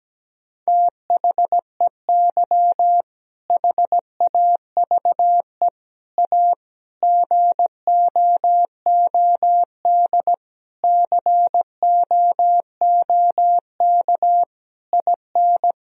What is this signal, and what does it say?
Morse code, heard as a keyed tone, about 17 wpm, 700 Hz. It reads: THEY HAVE A GOOD COOK IN
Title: Sentences from Top 1000 Words - Review Part 4 - 17wpm